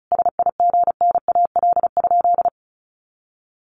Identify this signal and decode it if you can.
Morse code, heard as a keyed tone, about 34 wpm, 700 Hz. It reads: SIGNAL?
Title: Sentences from Top 1400 Words 35wpm